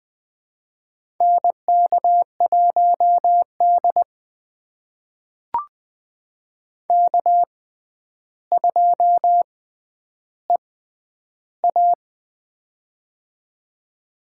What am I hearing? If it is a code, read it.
Morse code, heard as a keyed tone, about 20 wpm, spaced as for 10 wpm, 700 Hz. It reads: NK1D K2EA
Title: USA Call Signs - 4 Characters - 6x Character Spacing - 20wpm